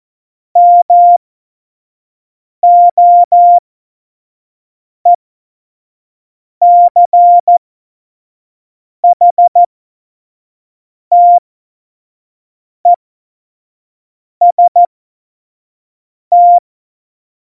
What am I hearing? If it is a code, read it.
Morse code, heard as a keyed tone, about 14 wpm, spaced as for 5 wpm, 700 Hz. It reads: MOECHTEST
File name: Morsekurs / Witze / Witz 10 April 2018, 25 F70